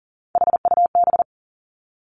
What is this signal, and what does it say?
Morse code, heard as a keyed tone, about 41 wpm, 700 Hz. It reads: HUB
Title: Three Challenging Letter-Number Combinations 40wpm